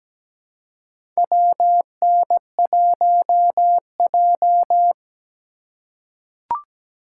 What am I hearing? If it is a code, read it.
Morse code, heard as a keyed tone, about 17 wpm, 700 Hz. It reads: WN1J
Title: USA Call Signs - 3x Character Spacing - 17wpm